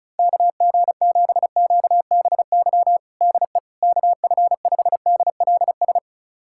Morse code, heard as a keyed tone, about 36 wpm, 700 Hz. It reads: KG7QBY DE KF5DLS
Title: USA Callsign DE Callsign - 6 Characters - 35wpm